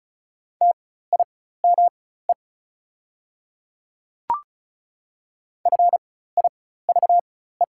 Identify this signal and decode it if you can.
Morse code, heard as a keyed tone, about 36 wpm, spaced as for 16 wpm, 690 Hz. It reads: TIME FIVE
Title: 4x Character Spacing - Top 500 Words - 35wpm